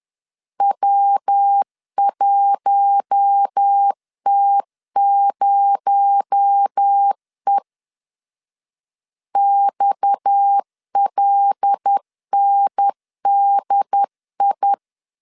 Morse code, heard as a keyed tone, about 10 wpm, 800 Hz. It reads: W1T0E XLNDI